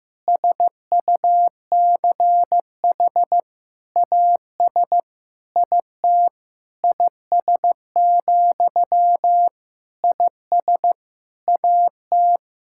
Morse code, heard as a keyed tone, about 15 wpm, 700 Hz. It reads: SUCH AS IT IS, IS AT
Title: Sentences from Top 1600 Words 15wpm